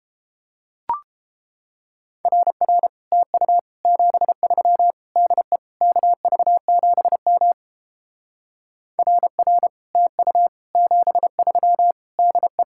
Morse code, heard as a keyed tone, about 31 wpm, 700 Hz. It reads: RR TU 73 DE KV7M RR TU 73 DE